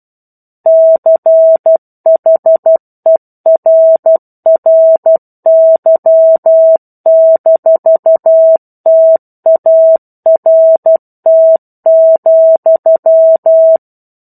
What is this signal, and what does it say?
Morse code, heard as a keyed tone, about 12 wpm, 650 Hz. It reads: CHERRY-TART,
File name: 01-Chapter-01 12 WPM